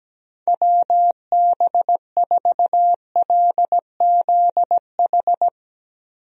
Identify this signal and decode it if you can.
Morse code, heard as a keyed tone, about 17 wpm, 700 Hz. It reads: WB4LZH
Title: USA Call Signs - 6 Characters - Repeated 3x - 17wpm